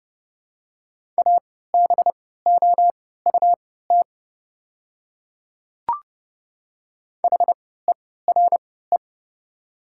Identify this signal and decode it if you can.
Morse code, heard as a keyed tone, about 30 wpm, spaced as for 17 wpm, 700 Hz. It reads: ABOUT HERE
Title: 3x Character Spacing - Top 500 Words - 30wpm